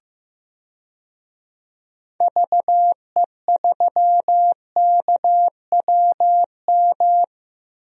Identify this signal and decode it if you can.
Morse code, heard as a keyed tone, about 15 wpm, 700 Hz. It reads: VE3KWM
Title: DXCC-100 Call Signs - 6 Characters - Repeated 3x - 15wpm